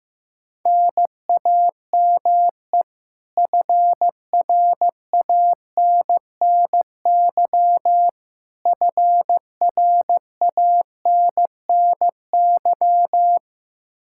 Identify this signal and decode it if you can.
Morse code, heard as a keyed tone, about 16 wpm, 700 Hz. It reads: NAME FRANNY FRANNY
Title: Practice QSO - Exchange 1 - Beginning - 15wpm